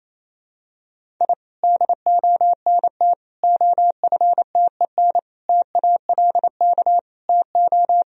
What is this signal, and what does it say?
Morse code, heard as a keyed tone, about 27 wpm, 700 Hz. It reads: I DONT OFTEN TALK TO
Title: Sentences from Top 1000 Words - Review Part 1 - 28wpm